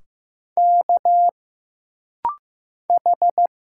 Morse code, heard as a keyed tone, about 15 wpm, 700 Hz. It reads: K H
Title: Single Letters - Rapid-Fire 15wpm